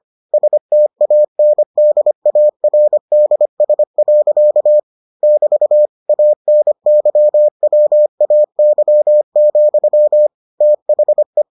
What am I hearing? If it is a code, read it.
Morse code, heard as a keyed tone, about 25 wpm, 600 Hz. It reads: STANDARDS. <BT> ANYWAY, THE